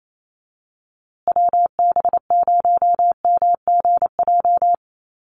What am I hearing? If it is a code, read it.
Morse code, heard as a keyed tone, about 28 wpm, 700 Hz. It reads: WB0MGJ
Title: USA Call Signs Repeated 2x 28wpm